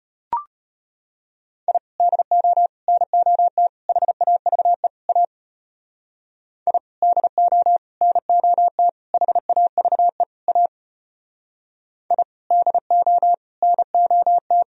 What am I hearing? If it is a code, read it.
Morse code, heard as a keyed tone, about 31 wpm, 700 Hz. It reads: I DO NOT HAVE A I DO NOT HAVE A I DO NOT
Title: Speed-Racing - Sets of 5 Words 25wpm